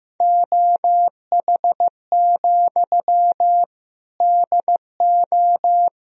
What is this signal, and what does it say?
Morse code, heard as a keyed tone, about 15 wpm, 700 Hz. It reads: OH, DO
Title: Sentences from Top 500 Words and Abbreviations 15wpm